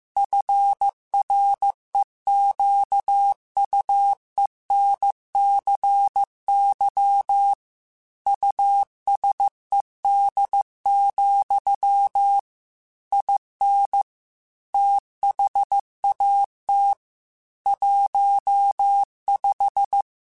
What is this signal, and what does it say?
Morse code, heard as a keyed tone, about 15 wpm, 800 Hz. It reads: FREQUENCY USED, IN THAT 15
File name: Technical Subjects / Technical Subjects-mp3s / Sw Pwr Sup @13